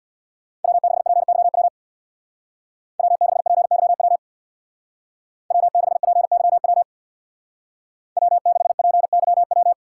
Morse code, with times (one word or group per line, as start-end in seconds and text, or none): 0.6-1.7 W6PYW
3.0-4.2 W6PYW
5.5-6.8 W6PYW
8.2-9.7 W6PYW